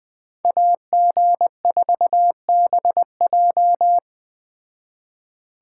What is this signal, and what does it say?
Morse code, heard as a keyed tone, about 20 wpm, 700 Hz. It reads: AG4BJ